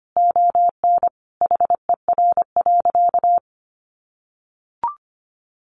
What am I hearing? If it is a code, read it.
Morse code, heard as a keyed tone, about 25 wpm, 700 Hz. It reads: ON HER.
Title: Sentences from Top 1500 Words 25wpm